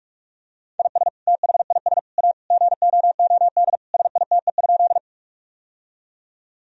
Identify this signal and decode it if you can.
Morse code, heard as a keyed tone, about 45 wpm, 700 Hz. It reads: IS THIS A GOOD SITE?
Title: Sentences from Top 1000 Words - Review Part 4 - 45wpm